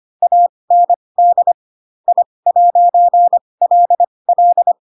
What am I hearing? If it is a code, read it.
Morse code, heard as a keyed tone, about 25 wpm, spaced as for 20 wpm, 700 Hz. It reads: AND I'LL
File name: Pinocchio / Pinocchio ch6 15 25 ik1hge